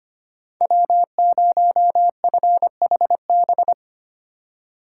Morse code, heard as a keyed tone, about 25 wpm, 700 Hz. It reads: W0FHB
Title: USA Call Signs - 5 Characters - 2x Character Spacing - 25wpm